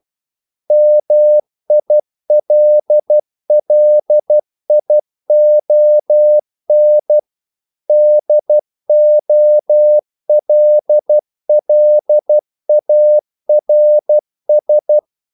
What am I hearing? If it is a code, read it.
Morse code, heard as a keyed tone, about 12 wpm, 600 Hz. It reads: MILLION DOLLARS
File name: CW / Practice / EnglishQuotations0312